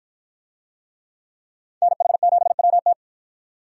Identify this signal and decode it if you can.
Morse code, heard as a keyed tone, about 53 wpm, 700 Hz. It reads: NH7WT